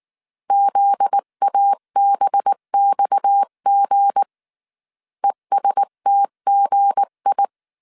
Morse code, heard as a keyed tone, about 19 wpm, 800 Hz. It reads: ZABXG ESTGI